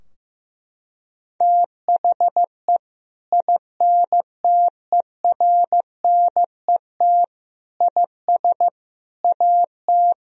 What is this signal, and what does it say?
Morse code, heard as a keyed tone, about 15 wpm, 700 Hz. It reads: THE INTERNET IS AT
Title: Sentences from Top 1000 Words - Review Part 4 - 15wpm